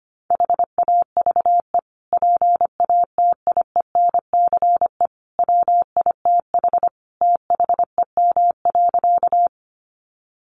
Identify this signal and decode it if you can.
Morse code, heard as a keyed tone, about 25 wpm, 700 Hz. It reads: HAVE PATIENCE WITH THEM.
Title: Sentences from Top 1300 Words 25wpm